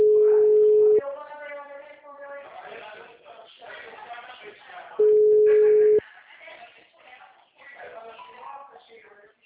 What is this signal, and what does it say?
Ringback tone (Europe, 425 Hz), with speech.
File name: out- -2019- - - .2301